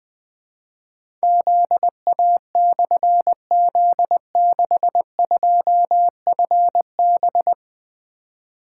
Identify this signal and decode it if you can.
Morse code, heard as a keyed tone, about 20 wpm, 700 Hz. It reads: ZA/Z62FB
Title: Difficult Call Signs 20wpm